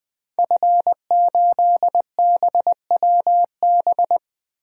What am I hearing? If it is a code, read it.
Morse code, heard as a keyed tone, about 20 wpm, 700 Hz. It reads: F8BWB
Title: DXCC-100 Call Signs - 5 Characters - 20wpm